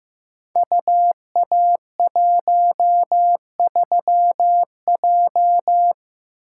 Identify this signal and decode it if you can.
Morse code, heard as a keyed tone, about 15 wpm, 700 Hz. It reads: UA13J
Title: Code Groups - 5 Letter:Numbers - 15wpm